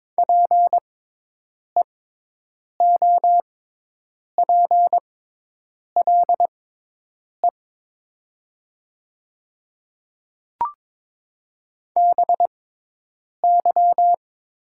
Morse code, heard as a keyed tone, about 22 wpm, spaced as for 8 wpm, 700 Hz. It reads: PEOPLE BY
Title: 6x Character Spacing - Top 300 Words - 22wpm